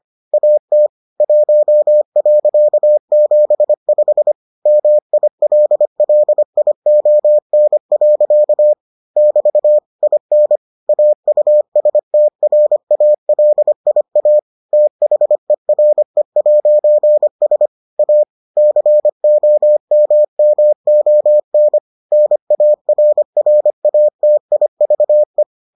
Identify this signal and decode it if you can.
Morse code, heard as a keyed tone, about 25 wpm, 600 Hz. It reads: AT 1.75 MILLION. <BT> IN AUSTRALIA THERE'S A COMMON NARRATIVE